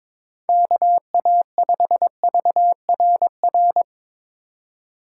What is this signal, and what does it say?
Morse code, heard as a keyed tone, about 21 wpm, 700 Hz. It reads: KA5VRR
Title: USA Call Signs - 6 Characters - 4x Character Spacing - 22wpm